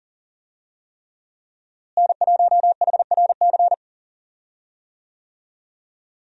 Morse code, heard as a keyed tone, about 37 wpm, 700 Hz. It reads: N1HRC
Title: USA Call Signs - 5 Characters - Repeated 3x - 40wpm